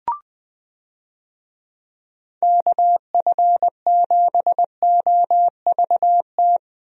Morse code, heard as a keyed tone, about 20 wpm, 700 Hz. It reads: KF7OVT
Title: USA Call Signs - 6 Characters - Repeated 2x - 20wpm